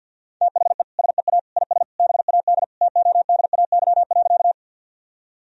Morse code, heard as a keyed tone, about 45 wpm, 700 Hz. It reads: THE SEA IS BAD TODAY.
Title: Sentences from Top 1000 Words - Review Part 1 - 50wpm